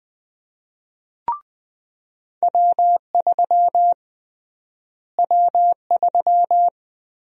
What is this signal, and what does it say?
Morse code, heard as a keyed tone, about 20 wpm, 700 Hz. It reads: W3 W3
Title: USA Call Sign Prefixes - Repeated 2x - 20wpm